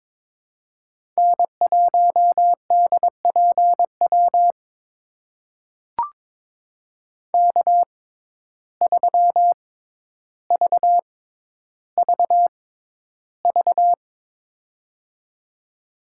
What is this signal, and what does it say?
Morse code, heard as a keyed tone, about 22 wpm, spaced as for 11 wpm, 700 Hz. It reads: N1DPW K3VVV
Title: USA Call Signs - 5 Characters - 6x Character Spacing - 22wpm